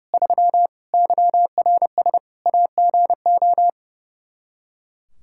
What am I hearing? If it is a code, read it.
Morse code, heard as a keyed tone, about 30 wpm, 700 Hz. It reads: 3 YRS AGO
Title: Practice QSO - Exchange 3 - More Info - 30wpm